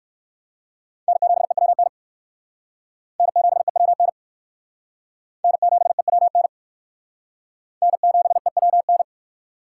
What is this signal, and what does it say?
Morse code, heard as a keyed tone, about 55 wpm, 700 Hz. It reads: N7EWN N7EWN N7EWN N7EWN
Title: Speed Racing - USA 1x3 Call Signs 45wpm